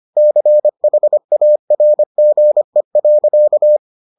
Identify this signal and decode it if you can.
Morse code, heard as a keyed tone, about 25 wpm, 600 Hz. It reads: CHARGE.